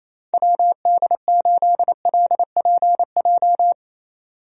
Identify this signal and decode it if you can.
Morse code, heard as a keyed tone, about 28 wpm, 700 Hz. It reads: WD8LPJ
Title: USA Call Signs - 6 Characters - 28wpm